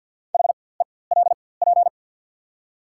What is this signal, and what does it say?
Morse code, heard as a keyed tone, about 53 wpm, spaced as for 24 wpm, 690 Hz. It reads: HELP
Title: 4x Character Spacing - Top 300 Words - 50wpm